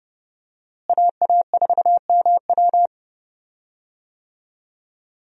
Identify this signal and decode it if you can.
Morse code, heard as a keyed tone, about 30 wpm, 700 Hz. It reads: AA4MW
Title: USA Call Signs - 5 Characters - Repeated 2x - 30wpm